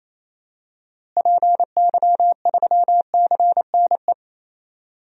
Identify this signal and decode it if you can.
Morse code, heard as a keyed tone, about 28 wpm, 700 Hz. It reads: PY3CNE
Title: DXCC-100 Call Signs - 6 Characters - Repeated 3x - 28wpm